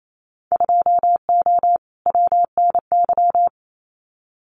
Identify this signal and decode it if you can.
Morse code, heard as a keyed tone, about 28 wpm, 700 Hz. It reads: 2O WNY
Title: Winter Field Day Exchanges Encore 28wpm